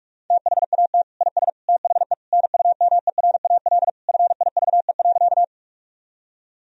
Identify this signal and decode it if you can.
Morse code, heard as a keyed tone, about 46 wpm, 700 Hz. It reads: THAT IS THE NUMERAL FIVE.
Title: Sentences from Top 1000 Words - Review Part 2 - 45wpm